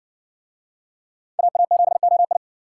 Morse code, heard as a keyed tone, about 57 wpm, 700 Hz. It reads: AA7OI